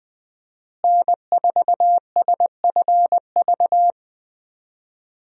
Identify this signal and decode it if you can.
Morse code, heard as a keyed tone, about 20 wpm, 700 Hz. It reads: N4SFV
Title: USA Call Signs - 5 Characters - Repeated 2x - 20wpm